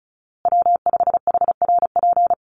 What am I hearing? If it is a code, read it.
Morse code, heard as a keyed tone, about 34 wpm, 700 Hz. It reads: W5HRP
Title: USA Call Signs Repeated 2x 35wpm